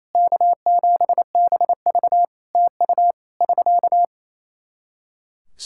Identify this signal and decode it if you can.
Morse code, heard as a keyed tone, about 27 wpm, 700 Hz. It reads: K7BV TU <SK>
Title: K1USN SST QSO Practice 28wpm